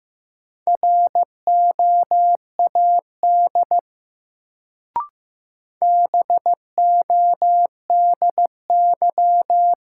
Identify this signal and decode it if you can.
Morse code, heard as a keyed tone, about 15 wpm, 700 Hz. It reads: ROAD BODY
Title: Sentences from Top 400 Words - New Word Drill - 15wpm